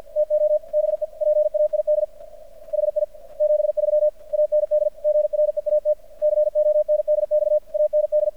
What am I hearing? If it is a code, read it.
Morse code, heard as a keyed tone, about 50 wpm, 600 Hz. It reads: TO BE PAID HI 73 AND GREAT CONDX AND
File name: archived / CRSnet